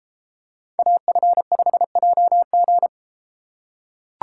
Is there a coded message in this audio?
Yes: Morse code from 0.8 to 2.9 s, heard as a keyed tone, about 34 wpm, 700 Hz. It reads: AF5JG